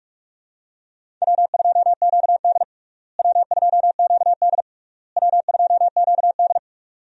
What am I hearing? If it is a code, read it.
Morse code, heard as a keyed tone, about 46 wpm, 700 Hz. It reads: W2QD W2QD W2QD
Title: USA Call Signs - 4 Characters - Repeated 3x - 45wpm